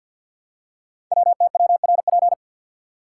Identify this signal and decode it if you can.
Morse code, heard as a keyed tone, about 48 wpm, 690 Hz. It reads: WTWRP
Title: Code Groups - 5 Letters - 50wpm